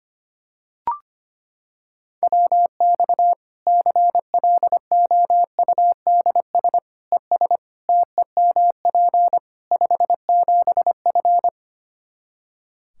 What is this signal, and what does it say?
Morse code, heard as a keyed tone, about 25 wpm, 700 Hz. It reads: WX CLOUDS ES TEMP 57F
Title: Practice QSO - Exchange 2 - Info - 25wpm